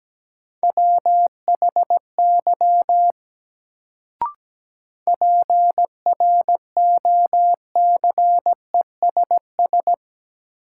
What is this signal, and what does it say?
Morse code, heard as a keyed tone, about 17 wpm, 700 Hz. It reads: WHY PROCESS
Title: Sets of 3 Words - New Word Drill - 17wpm